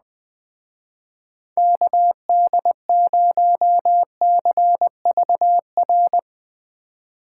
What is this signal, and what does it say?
Morse code, heard as a keyed tone, about 20 wpm, 700 Hz. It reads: KD0CVR